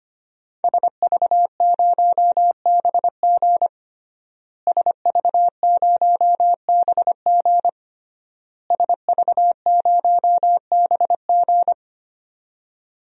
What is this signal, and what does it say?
Morse code, heard as a keyed tone, about 25 wpm, 700 Hz. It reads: SV0BG SV0BG SV0BG
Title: DXCC-100 Call Signs - 5 Characters - Repeated 3x - 25wpm